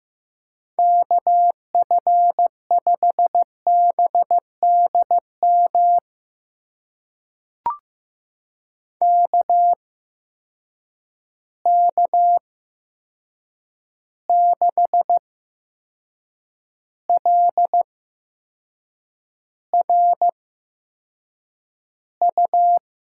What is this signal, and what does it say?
Morse code, heard as a keyed tone, about 15 wpm, spaced as for 6 wpm, 700 Hz. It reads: KF5BDM KK6LRU
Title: USA Call Signs - 6 Characters - 8x Character Spacing - 15wpm